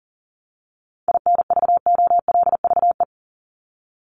Morse code, heard as a keyed tone, about 40 wpm, 700 Hz. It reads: INVOLVE
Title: ED Suffix - Top 100 Words with Base Words 40wpm